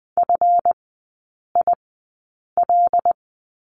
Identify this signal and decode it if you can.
Morse code, heard as a keyed tone, about 20 wpm, spaced as for 8 wpm, 700 Hz. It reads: FIL
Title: 198 Lesson 32e - Letter k - Three Characters 20wpm